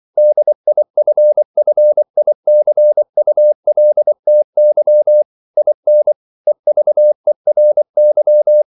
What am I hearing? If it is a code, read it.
Morse code, heard as a keyed tone, about 24 wpm, 600 Hz. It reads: DIFFICULTY IN EVERY
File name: CW / Practice / EnglishQuotations0424